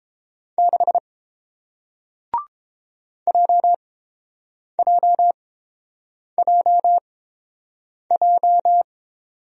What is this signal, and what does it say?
Morse code, heard as a keyed tone, about 30 wpm, spaced as for 7 wpm, 700 Hz. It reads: 6 JJJJ